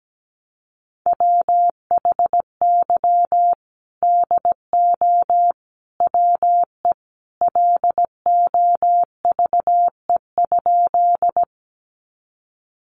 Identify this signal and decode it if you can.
Morse code, heard as a keyed tone, about 17 wpm, 700 Hz. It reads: WHY DO WE LOVE?
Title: Sentences from Top 400 Words 17wpm